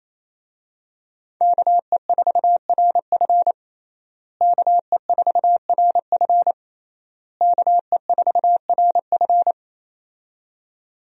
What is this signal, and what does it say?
Morse code, heard as a keyed tone, about 28 wpm, 700 Hz. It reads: KE4RF KE4RF KE4RF
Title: USA Call Signs - 5 Characters - Repeated 3x - 28wpm